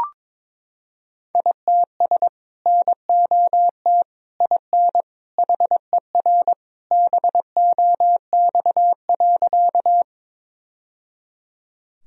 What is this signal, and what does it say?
Morse code, heard as a keyed tone, about 22 wpm, 700 Hz. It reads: ITS NOT IN HER BOX.
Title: Sentences from Top 1000 Words - Review Part 2 - 22wpm